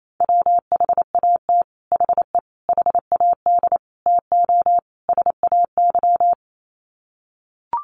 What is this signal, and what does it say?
Morse code, heard as a keyed tone, about 28 wpm, 700 Hz. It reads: WHAT HE HAD TO SAY
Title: Sets of 5 Words Encore2 28wpm